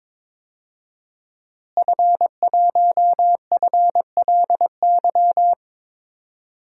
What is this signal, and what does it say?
Morse code, heard as a keyed tone, about 22 wpm, 700 Hz. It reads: F1FLY
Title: DXCC-100 Call Signs - 5 Characters - Repeated 3x - 22wpm